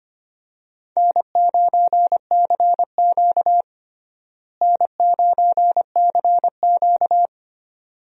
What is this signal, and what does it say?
Morse code, heard as a keyed tone, about 25 wpm, 700 Hz. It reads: N9CQ N9CQ
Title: USA Call Signs - 4 Characters - Repeated 2x - 25wpm